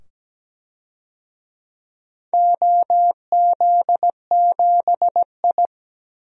Morse code, heard as a keyed tone, about 17 wpm, 700 Hz. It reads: OZ7I
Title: DXCC-100 Call Signs - 4 Characters - 17wpm